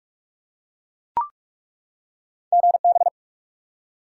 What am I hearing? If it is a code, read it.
Morse code, heard as a keyed tone, about 42 wpm, 700 Hz. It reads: GB
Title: Top 90 QSO Elements 45wpm